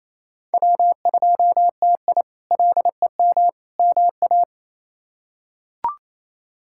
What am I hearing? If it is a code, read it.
Morse code, heard as a keyed tone, about 27 wpm, 700 Hz. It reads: W2TI LEM MA
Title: K1USN SST QSO Practice - 28wpm